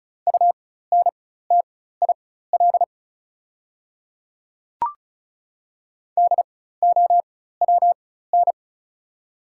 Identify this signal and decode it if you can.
Morse code, heard as a keyed tone, about 36 wpm, spaced as for 16 wpm, 700 Hz. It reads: UNTIL DOWN
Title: 4x Character Spacing - Top 500 Words - 35wpm